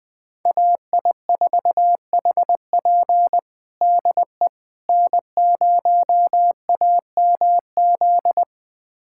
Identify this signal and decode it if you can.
Morse code, heard as a keyed tone, about 20 wpm, 700 Hz. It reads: AI4HP DE N0AMZ
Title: USA Callsign DE Callsign - 5 Characters - 20wpm